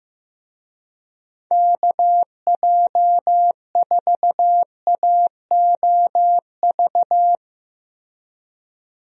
Morse code, heard as a keyed tone, about 15 wpm, 700 Hz. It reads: KJ4AOV
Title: USA Call Signs - 6 Characters - Repeated 2x - 15wpm